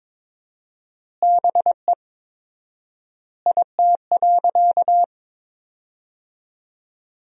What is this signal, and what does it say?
Morse code, heard as a keyed tone, about 22 wpm, 700 Hz. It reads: BE IT.
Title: 4x Word-Spacing - Sentences from Top 100 Words - 22wpm